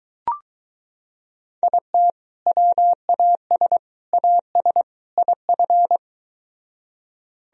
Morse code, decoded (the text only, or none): IT WAS AS IF